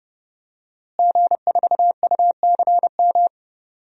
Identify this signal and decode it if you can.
Morse code, heard as a keyed tone, about 30 wpm, 700 Hz. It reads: G4UCM
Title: DXCC-100 Call Signs - 5 Characters - Repeated 2x - 30wpm